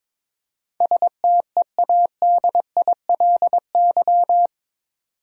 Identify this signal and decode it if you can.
Morse code, heard as a keyed tone, about 22 wpm, 700 Hz. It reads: STEADILY
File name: practice / 22wpm / Eight Letter Words - 22wpm